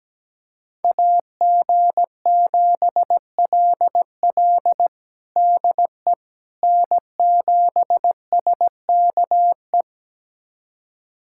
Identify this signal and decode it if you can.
Morse code, heard as a keyed tone, about 17 wpm, 700 Hz. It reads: AG7LL DE N7SKE
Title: USA Callsign DE Callsign - 5 Characters - 17wpm